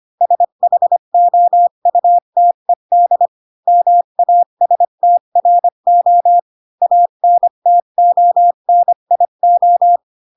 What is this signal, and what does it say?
Morse code, heard as a keyed tone, about 25 wpm, 700 Hz. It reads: SHOUTED MASTRO ANTONIO